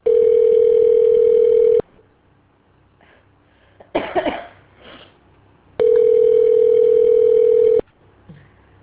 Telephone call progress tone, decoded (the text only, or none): ringback tone